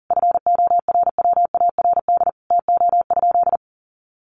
Morse code, heard as a keyed tone, about 40 wpm, 700 Hz. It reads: FORWARD TO?
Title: Sentences from Top 1300 Words 40wpm